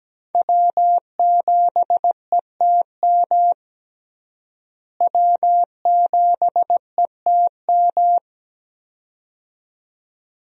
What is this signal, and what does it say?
Morse code, heard as a keyed tone, about 17 wpm, 700 Hz. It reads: W7ETM W7ETM
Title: USA Call Signs - 5 Characters - Repeated 2x - 17wpm